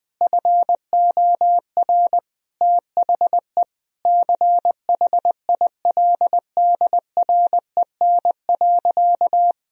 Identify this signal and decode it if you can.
Morse code, heard as a keyed tone, about 20 wpm, 700 Hz. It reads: FOR THE CHILDREN.
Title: Sentences from Top 1000 Words - Review Part 2 - 20wpm